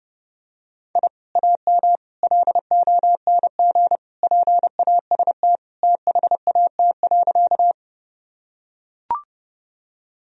Morse code, heard as a keyed tone, about 30 wpm, 700 Hz. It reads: I AM LONG PAST THAT.
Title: Sentences from Top 1000 Words - Review Part 3 - 30wpm